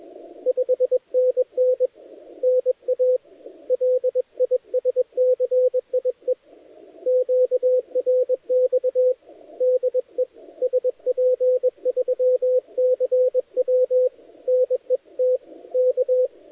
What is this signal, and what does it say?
Morse code, heard as a keyed tone, about 21 wpm, 510 Hz, with radio noise. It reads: 5NN NA LISCIE QRX DE SP3CW NET K
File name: dlazaawansowanych / nagraniazpasma / titawaNETcontrol